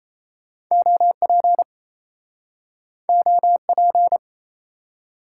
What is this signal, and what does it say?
Morse code, heard as a keyed tone, about 39 wpm, 700 Hz. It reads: OP OP